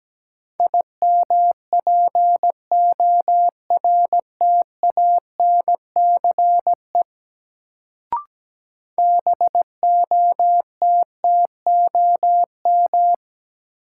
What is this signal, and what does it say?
Morse code, heard as a keyed tone, about 17 wpm, 700 Hz. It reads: IMPORTANCE BOTTOM
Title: Sets of 3 Words - New Word Drill - 17wpm